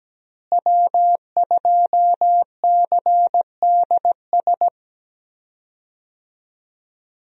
Morse code, heard as a keyed tone, about 17 wpm, 700 Hz. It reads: W2CDS
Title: USA Call Signs - 5 Characters - Repeated 3x - 17wpm